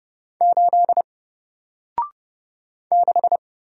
Morse code, heard as a keyed tone, about 30 wpm, 700 Hz. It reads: 8 6
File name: Single-Letter-Number-30wpm